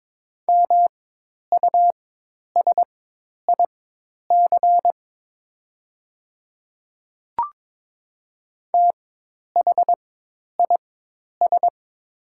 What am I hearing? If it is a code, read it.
Morse code, heard as a keyed tone, about 22 wpm, spaced as for 10 wpm, 700 Hz. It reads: MUSIC THIS
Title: 4x Character Spacing - Top 300 Words - 22wpm